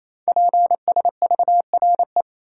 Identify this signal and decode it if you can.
Morse code, heard as a keyed tone, about 27 wpm, 700 Hz. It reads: PSVRE
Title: Code Groups - 5 Letter:Numbers - 28wpm